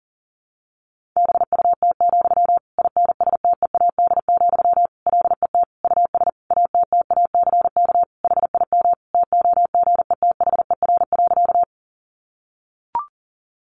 Morse code, heard as a keyed tone, about 40 wpm, 700 Hz. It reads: BUT, INSTEAD, LET US ATTACK HIM TOGETHER.
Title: Sentences from Top 1500 Words 40wpm